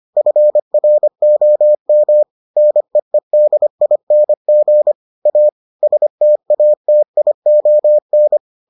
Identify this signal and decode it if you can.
Morse code, heard as a keyed tone, about 25 wpm, 600 Hz. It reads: FROM NEEDING A STATION